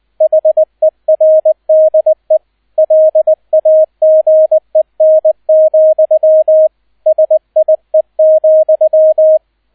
Morse code, heard as a keyed tone, about 20 wpm, 630 Hz, with radio noise. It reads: HERDE LAGEN, SIE,